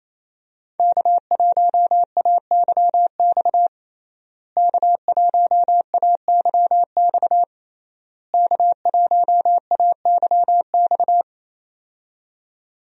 Morse code, heard as a keyed tone, about 28 wpm, 700 Hz. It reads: K1AYX K1AYX K1AYX
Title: USA Call Signs - 5 Characters - Repeated 3x - 28wpm